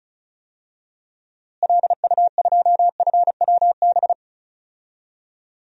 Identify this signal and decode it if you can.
Morse code, heard as a keyed tone, about 36 wpm, 700 Hz. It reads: LU2FWB
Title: DXCC-100 Call Signs - 6 Characters - Repeated 2x - 35wpm